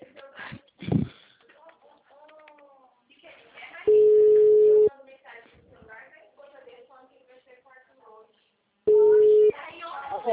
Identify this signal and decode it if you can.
Ringback tone (Europe, 425 Hz), with speech.
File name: out- -2026- - - .96765